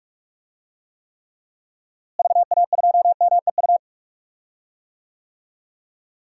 Morse code, heard as a keyed tone, about 41 wpm, 700 Hz. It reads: VA2MEU
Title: DXCC-100 Call Signs - 6 Characters - Repeated 3x - 45wpm